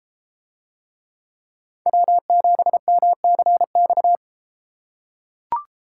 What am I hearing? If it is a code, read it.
Morse code, heard as a keyed tone, about 32 wpm, 700 Hz. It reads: W7MCX